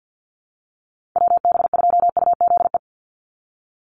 Morse code, heard as a keyed tone, about 50 wpm, 700 Hz. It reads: WB2UZE
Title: Call Signs - 50wpm